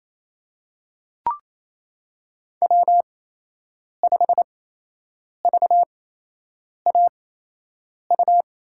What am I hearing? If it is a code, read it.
Morse code, heard as a keyed tone, about 28 wpm, spaced as for 8 wpm, 700 Hz. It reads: W5VAU
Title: USA Call Signs - 5 Characters - 8x Character Spacing - 28wpm